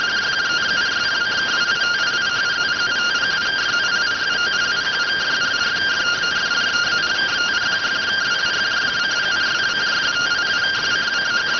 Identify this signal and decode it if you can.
Two-tone teleprinter signal, about 170 Hz shift.